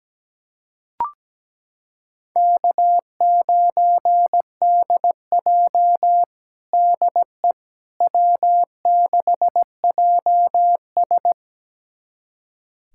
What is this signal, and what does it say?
Morse code, heard as a keyed tone, about 17 wpm, 700 Hz. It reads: K9DJ DE W6JS
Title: USA Callsign DE Callsign - 4 Characters - 17wpm